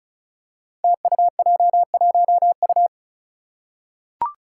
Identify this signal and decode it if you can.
Morse code, heard as a keyed tone, about 36 wpm, 700 Hz. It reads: TUJ1U
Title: Five Challenging Letter-Number Combinations 35wpm